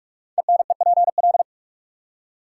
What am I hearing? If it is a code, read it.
Morse code, heard as a keyed tone, about 44 wpm, 690 Hz. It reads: ENEPL
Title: Code Groups - 5 Letters - 45wpm